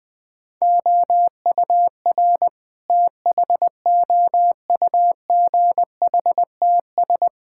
Morse code, heard as a keyed tone, about 20 wpm, 700 Hz. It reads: OUR THOUGHTS
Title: Sentences with 8-Word Phrases - Spoken Beginning - 20wpm